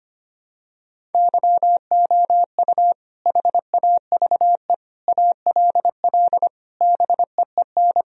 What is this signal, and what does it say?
Morse code, heard as a keyed tone, about 25 wpm, 700 Hz. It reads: YOU HAVE ALL BEEN
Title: Sentences from Top 1000 Words - Review Part 2 - 25wpm